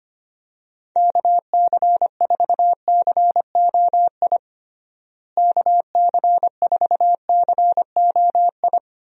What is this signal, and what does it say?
Morse code, heard as a keyed tone, about 25 wpm, 700 Hz. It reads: KC4COI KC4COI
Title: USA Call Signs - 6 Characters - Repeated 2x - 25wpm